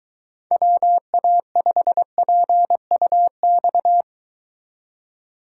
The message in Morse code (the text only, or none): WA5PUX